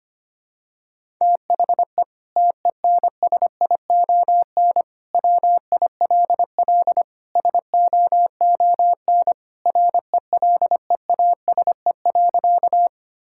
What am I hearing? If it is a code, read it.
Morse code, heard as a keyed tone, about 25 wpm, 700 Hz. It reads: THE TENSION WILL SOON RELEASE.
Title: Sentences from Top 1600 Words 25wpm